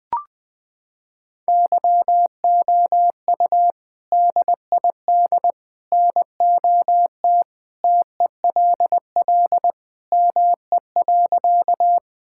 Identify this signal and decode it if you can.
Morse code, heard as a keyed tone, about 20 wpm, 700 Hz. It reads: YOU DID NOT TELL ME.
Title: Sentences with 3-Word Phrases - 20wpm